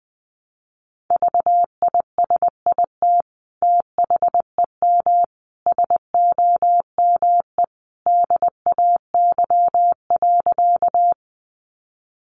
Morse code, heard as a keyed tone, about 20 wpm, 700 Hz. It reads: VISIT THEM SOME DAY.
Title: Sentences from Top 900 Words 20wpm